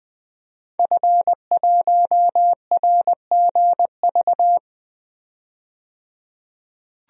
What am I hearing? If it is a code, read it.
Morse code, heard as a keyed tone, about 20 wpm, 700 Hz. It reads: F1RGV
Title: DXCC-100 Call Signs - 5 Characters - 20wpm